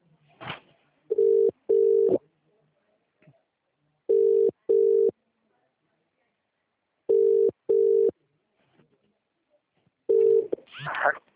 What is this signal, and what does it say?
Ringback tone (UK), 400 + 450 Hz.